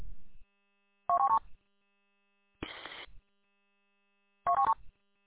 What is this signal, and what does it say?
There is no code.